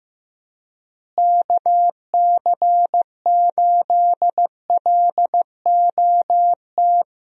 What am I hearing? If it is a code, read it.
Morse code, heard as a keyed tone, about 15 wpm, 700 Hz. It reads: KC8LOT
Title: USA Call Signs - 6 Characters - Repeated 2x - 15wpm